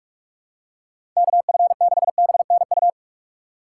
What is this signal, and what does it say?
Morse code, heard as a keyed tone, about 46 wpm, 700 Hz. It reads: KF6BNU
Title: USA Call Signs - 3x Character Spacing - 45wpm